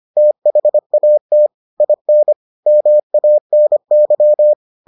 Morse code, heard as a keyed tone, about 25 wpm, 600 Hz. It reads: THAT IN MANY